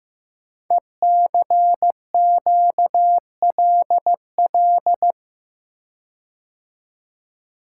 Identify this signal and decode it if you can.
Morse code, heard as a keyed tone, about 15 wpm, 700 Hz. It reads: ECQLL
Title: Code Groups - 5 Letters - 15wpm